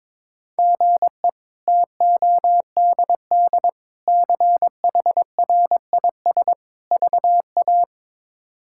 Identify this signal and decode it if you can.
Morse code, heard as a keyed tone, about 21 wpm, 700 Hz. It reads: GE TODD CHRIS VA
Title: K1USN SST QSO Practice - 22wpm